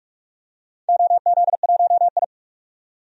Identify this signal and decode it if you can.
Morse code, heard as a keyed tone, about 45 wpm, 700 Hz. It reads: OZ1I
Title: DXCC-100 Call Signs - 4 Characters - 45wpm